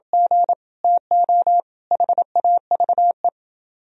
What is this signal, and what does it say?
Morse code, heard as a keyed tone, about 26 wpm, 700 Hz. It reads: G TO HAVE